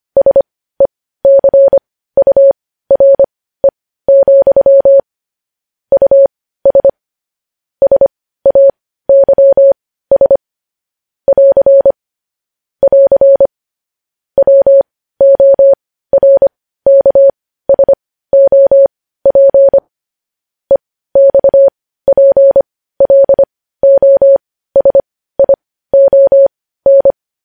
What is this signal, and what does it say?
Morse code, heard as a keyed tone, about 25 wpm, spaced as for 15 wpm, 550 Hz. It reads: SECURE, US SAYS <AR> <AR> WORKSOP EXPLOSION
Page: news-headlines-in-morse-code-at-15-wpm